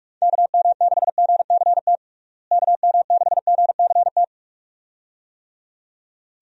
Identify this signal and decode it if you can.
Morse code, heard as a keyed tone, about 46 wpm, 700 Hz. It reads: KM6GCT KM6GCT
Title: USA Call Signs - 6 Characters - Repeated 2x - 45wpm